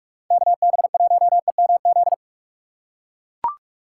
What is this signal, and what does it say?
Morse code, heard as a keyed tone, about 44 wpm, 700 Hz. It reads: KB1EMZ
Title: USA Call Signs - 6 Characters - 6x Character Spacing - 45wpm